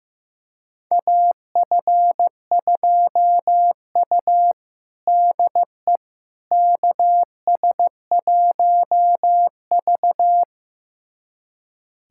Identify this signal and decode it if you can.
Morse code, heard as a keyed tone, about 16 wpm, 700 Hz. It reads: AF2U DE KS1V
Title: USA Callsign DE Callsign - 4 Characters - 15wpm